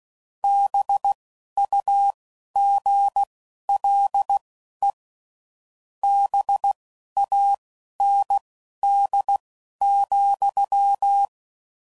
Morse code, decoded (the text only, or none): BUGLE BAND,